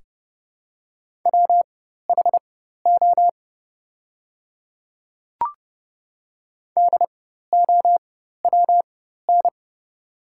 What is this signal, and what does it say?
Morse code, heard as a keyed tone, about 30 wpm, spaced as for 14 wpm, 700 Hz. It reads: WHO DOWN
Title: 4x Character Spacing - Top 200 Words - 30wpm